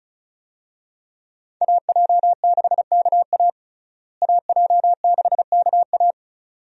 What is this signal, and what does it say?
Morse code, heard as a keyed tone, about 36 wpm, 700 Hz. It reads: AJ6KA AJ6KA
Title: USA Call Signs - 5 Characters - Repeated 2x - 35wpm